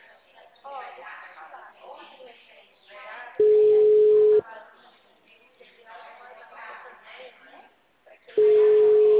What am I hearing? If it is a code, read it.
Ringback tone (Europe, 425 Hz), with speech.